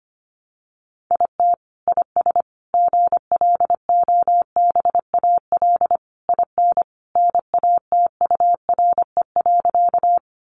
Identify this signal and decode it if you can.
Morse code, heard as a keyed tone, about 25 wpm, 700 Hz. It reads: IT IS GLOBAL IN NATURE.
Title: Sentences from Top 1200 Words 25wpm